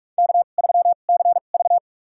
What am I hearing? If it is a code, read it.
Morse code, heard as a keyed tone, about 46 wpm, spaced as for 32 wpm, 700 Hz. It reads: K3CV
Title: USA Call Signs - 4 Characters - 2x Character Spacing - 45wpm